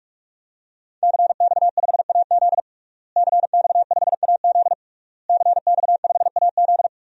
Morse code, heard as a keyed tone, about 41 wpm, 700 Hz. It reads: CX5AZ CX5AZ CX5AZ
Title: DXCC-100 Call Signs - 5 Characters - Repeated 3x - 45wpm